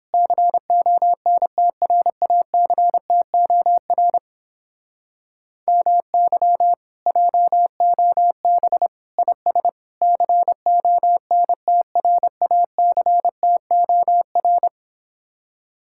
Morse code, heard as a keyed tone, about 29 wpm, 700 Hz. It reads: CONTRACTOR MY JOB IS CONTRACTOR